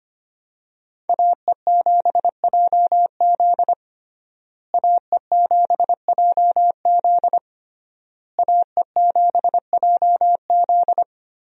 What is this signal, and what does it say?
Morse code, heard as a keyed tone, about 25 wpm, 700 Hz. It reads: AE7JZ AE7JZ AE7JZ
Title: USA Call Signs - 5 Characters - Repeated 3x - 25wpm